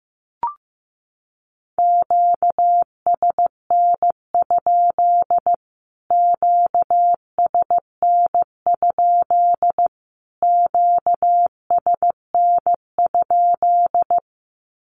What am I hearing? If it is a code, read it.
Morse code, heard as a keyed tone, about 15 wpm, 700 Hz. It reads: QSN? QSN? QSN?
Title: Q-Codes Repeated 3x 15wpm